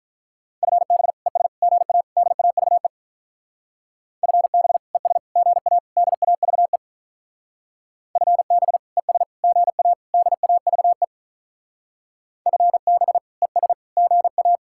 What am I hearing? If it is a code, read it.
Morse code, heard as a keyed tone, about 45 wpm, 700 Hz. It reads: FB ES GA DAVE FB ES GA DAVE FB ES GA DAVE FB ES GA